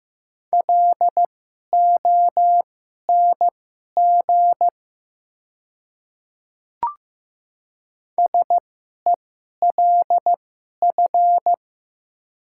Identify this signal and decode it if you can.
Morse code, heard as a keyed tone, about 15 wpm, spaced as for 11 wpm, 700 Hz. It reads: LONG SELF
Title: 2x Character Spacing - Top 200 Words - 15wpm